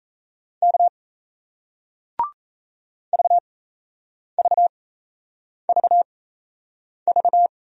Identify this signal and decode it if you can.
Morse code, heard as a keyed tone, about 38 wpm, spaced as for 8 wpm, 700 Hz. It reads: K VVVV